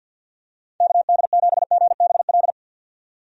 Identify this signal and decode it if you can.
Morse code, heard as a keyed tone, about 51 wpm, 700 Hz. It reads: KD7GBL
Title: USA Call Signs - 6 Characters - 50wpm